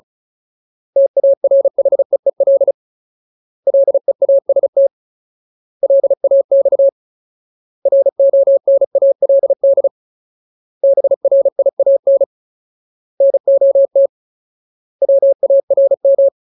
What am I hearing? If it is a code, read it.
Morse code, heard as a keyed tone, about 34 wpm, 550 Hz. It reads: TARHEEL LEAST LAX RONALD BRIAN NOT WARM
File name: Ultimate Word List / 35WPM Ultimate